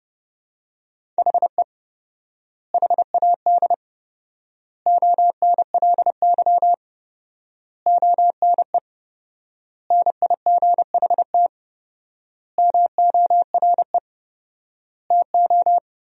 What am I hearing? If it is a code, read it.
Morse code, heard as a keyed tone, about 30 wpm, 700 Hz. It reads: HE HAD ONLY ONE NIGHT MORE TO
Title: 4x Word-Spacing - Sentences from Top 300 Words - 30wpm